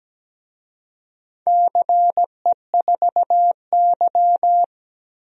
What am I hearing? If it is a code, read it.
Morse code, heard as a keyed tone, about 17 wpm, 700 Hz. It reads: CE4Y
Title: DXCC-100 Call Signs - 4 Characters - 17wpm